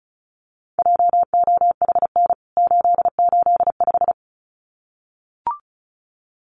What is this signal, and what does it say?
Morse code, heard as a keyed tone, about 34 wpm, 700 Hz. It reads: JOHN 885
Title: CWT Exchange 35wpm